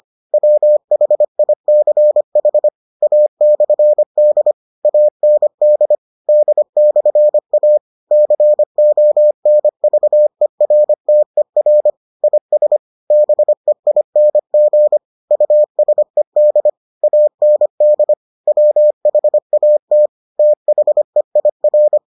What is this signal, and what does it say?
Morse code, heard as a keyed tone, about 25 wpm, 600 Hz. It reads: WHICH A/D AND D/A CONVERTER IS BEING USED AND WHAT THEIR